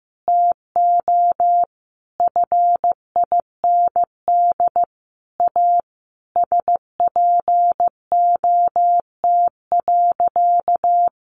Morse code, heard as a keyed tone, about 15 wpm, 700 Hz. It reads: TO FIND A SPOT.
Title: Sentences from Top 1100 Words 15wpm